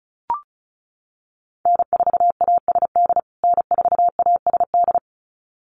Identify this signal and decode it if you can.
Morse code, heard as a keyed tone, about 34 wpm, 700 Hz. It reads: N4ASD N4ASD
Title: USA Call Signs Repeated 2x 35wpm